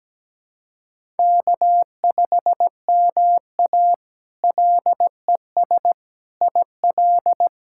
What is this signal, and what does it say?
Morse code, heard as a keyed tone, about 17 wpm, 700 Hz. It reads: K5MA LES IL
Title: K1USN SST QSO Practice - 17wpm